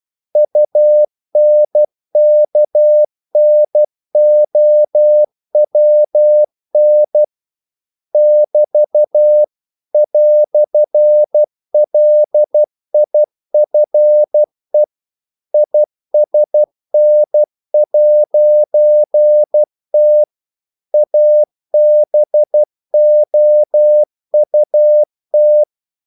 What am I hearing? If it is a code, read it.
Morse code, heard as a keyed tone, about 12 wpm, 600 Hz. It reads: UNKNOWN <BT> "LIFE ISN'T ABOUT
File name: CW / Practice / EnglishQuotations0112